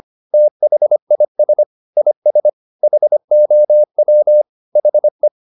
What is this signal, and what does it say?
Morse code, heard as a keyed tone, about 25 wpm, 600 Hz. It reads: THIS IS HOW HE
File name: the old man and the sea part 1 25wpm